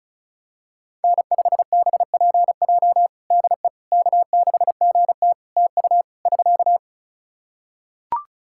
Morse code, heard as a keyed tone, about 34 wpm, 700 Hz. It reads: N5BPJ DE K6GT TU <SK>
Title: K1USN SST QSO Practice 35wpm